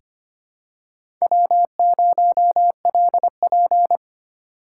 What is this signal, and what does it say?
Morse code, heard as a keyed tone, about 25 wpm, 700 Hz. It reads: W0LP
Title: USA Call Signs - 4 Characters - 8x Character Spacing - 25wpm